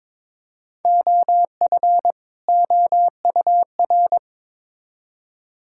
Morse code, heard as a keyed tone, about 21 wpm, 700 Hz. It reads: OF OUR
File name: Build A Phrase - 2-Word Phrases - 22wpm